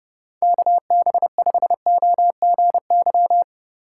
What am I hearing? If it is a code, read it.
Morse code, heard as a keyed tone, about 30 wpm, 700 Hz. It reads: KB5OGY